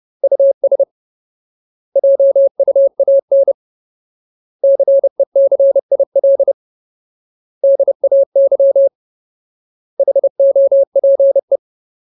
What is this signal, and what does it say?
Morse code, heard as a keyed tone, about 30 wpm, 550 Hz. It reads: US JUAN CECIL DAY HOPE
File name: Ultimate Word List / 30WPM Ultimate